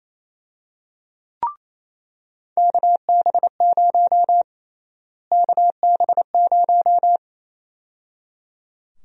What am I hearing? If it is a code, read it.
Morse code, heard as a keyed tone, about 28 wpm, 700 Hz. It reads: KB0 KB0
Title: USA Call Sign Prefixes - Repeated 2x - 28wpm